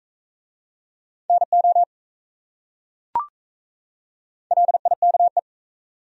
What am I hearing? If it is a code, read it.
Morse code, heard as a keyed tone, about 37 wpm, 700 Hz. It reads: NO LIKE